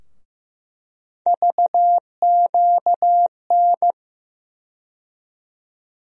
Morse code, heard as a keyed tone, about 15 wpm, 700 Hz. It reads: VQN
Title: USA Call Sign Postfixes - 3 Letters 15wpm